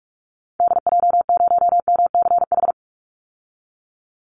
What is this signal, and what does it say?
Morse code, heard as a keyed tone, about 44 wpm, 700 Hz. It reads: DJ0MCH
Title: Call Signs Repeated 2x - 45wpm